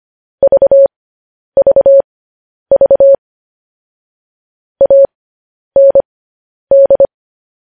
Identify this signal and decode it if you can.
Morse code, heard as a keyed tone, about 25 wpm, spaced as for 10 wpm, 550 Hz. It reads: VVV AND
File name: podcast / News headlines-1x-25x10